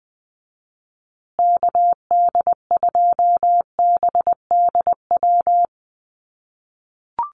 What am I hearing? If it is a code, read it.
Morse code, heard as a keyed tone, about 20 wpm, 700 Hz. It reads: KD2BDW
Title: USA Call Signs Repeated 3x 20wpm